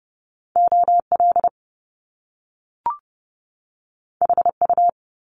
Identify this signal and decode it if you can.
Morse code, heard as a keyed tone, about 30 wpm, 700 Hz. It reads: OL HU
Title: Top Two Letter Combinations in Words 30wpm